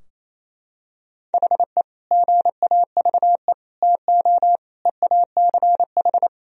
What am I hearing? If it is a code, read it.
Morse code, heard as a keyed tone, about 28 wpm, 700 Hz. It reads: HE GAVE TO EACH
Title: Sentences from Top 1000 Words - Review Part 4 - 28wpm